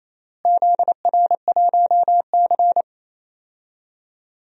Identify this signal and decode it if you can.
Morse code, heard as a keyed tone, about 28 wpm, 700 Hz. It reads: ZR1C
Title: DXCC-100 Call Signs - 4 Characters - Repeated 3x - 28wpm